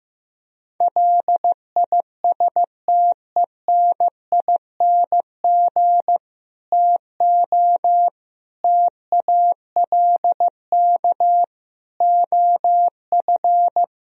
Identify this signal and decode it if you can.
Morse code, heard as a keyed tone, about 15 wpm, 700 Hz. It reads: LISTENING TO TALK OF
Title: Speed-Racing - Sets of 5 Words 15wpm